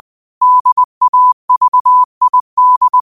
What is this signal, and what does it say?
Morse code, heard as a keyed tone, about 20 wpm, 1000 Hz. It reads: DAVID
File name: qotdfiles / 20wpm